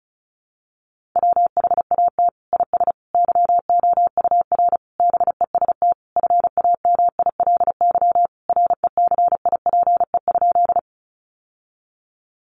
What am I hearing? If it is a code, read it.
Morse code, heard as a keyed tone, about 34 wpm, 700 Hz. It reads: WHAT IS YOUR BEST FAMILY RECIPE?
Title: Sentences from Top 1200 Words 35wpm